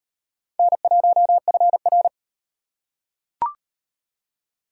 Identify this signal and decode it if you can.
Morse code, heard as a keyed tone, about 37 wpm, 700 Hz. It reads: N1FR